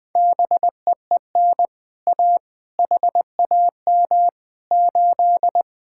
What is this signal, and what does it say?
Morse code, heard as a keyed tone, about 20 wpm, 700 Hz. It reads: BEEN A HAM 8
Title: Speed-Racing Common QSO Phrases Encore - 17wpm